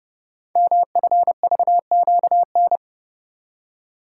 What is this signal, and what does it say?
Morse code, heard as a keyed tone, about 30 wpm, 700 Hz. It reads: MFVQN
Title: Code Groups - 5 Letters - 30wpm